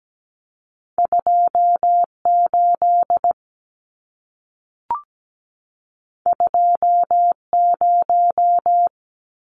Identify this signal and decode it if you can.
Morse code, heard as a keyed tone, about 17 wpm, 700 Hz. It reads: 28 20
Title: Numbers - 2-Digit 17wpm